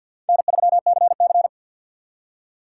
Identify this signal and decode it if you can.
Morse code, heard as a keyed tone, about 65 wpm, 700 Hz. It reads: N3CC